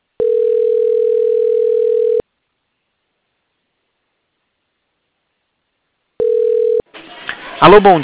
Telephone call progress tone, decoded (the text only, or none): ringback tone